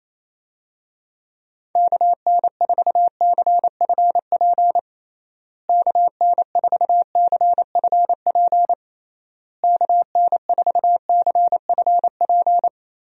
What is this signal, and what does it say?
Morse code, heard as a keyed tone, about 28 wpm, 700 Hz. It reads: KN4CFP KN4CFP KN4CFP
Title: USA Call Signs - 6 Characters - Repeated 3x - 28wpm